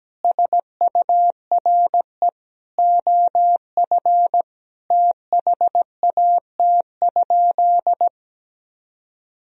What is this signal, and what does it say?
Morse code, heard as a keyed tone, about 17 wpm, 700 Hz. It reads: SURE OF THAT?
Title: Sentences from Top 300 Words and Abbreviations 17wpm